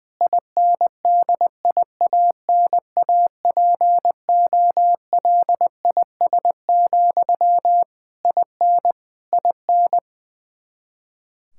Morse code, heard as a keyed tone, about 20 wpm, 700 Hz. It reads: INDIANAPOLIS, IN IN
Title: Practice QSO - Exchange 1 - Beginning - 20wpm